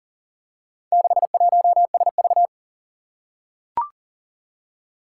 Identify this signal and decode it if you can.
Morse code, heard as a keyed tone, about 40 wpm, 700 Hz. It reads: 61SV
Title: Four Challenging Letter-Number Combinations 40wpm